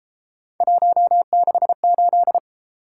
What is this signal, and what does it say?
Morse code, heard as a keyed tone, about 33 wpm, 700 Hz. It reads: 168